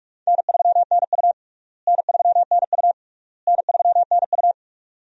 Morse code, heard as a keyed tone, about 41 wpm, 700 Hz. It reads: N3NU N3NU N3NU
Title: USA Call Signs - 4 Characters - Repeated 3x - 45wpm